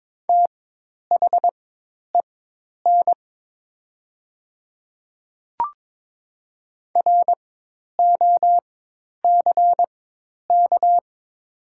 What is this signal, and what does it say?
Morse code, heard as a keyed tone, about 22 wpm, spaced as for 10 wpm, 700 Hz. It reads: THEN ROCK
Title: 4x Character Spacing - Top 500 Words - 22wpm